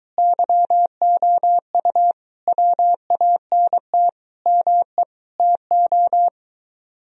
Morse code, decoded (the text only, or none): YOU WANT ME TO